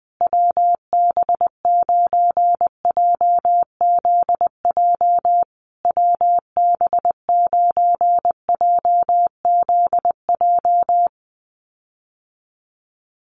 Morse code, heard as a keyed tone, about 19 wpm, 700 Hz. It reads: WB9JZJ WB9JZJ
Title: USA Call Signs Repeated 3x 20wpm